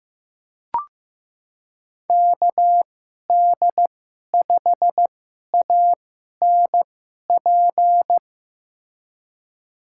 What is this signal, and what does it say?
Morse code, heard as a keyed tone, about 15 wpm, spaced as for 11 wpm, 700 Hz. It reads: KD5ANP
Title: USA Call Signs - 6 Characters - 2x Character Spacing - 15wpm